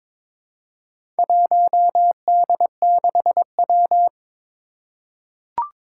Morse code, heard as a keyed tone, about 22 wpm, 700 Hz. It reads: 1D6W
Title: Four Challenging Letter-Number Combinations 22wpm